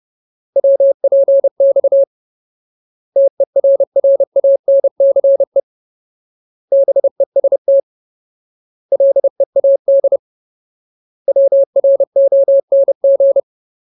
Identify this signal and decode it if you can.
Morse code, heard as a keyed tone, about 30 wpm, 550 Hz. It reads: WPX TERRANCE BEST LEAD WRONG
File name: Ultimate Word List / 30WPM Ultimate